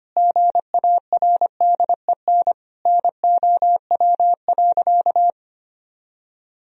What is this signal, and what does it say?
Morse code, heard as a keyed tone, about 25 wpm, 700 Hz. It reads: GARDEN NOW.
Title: Sentences from Top 1000 Words - Review Part 4 - 25wpm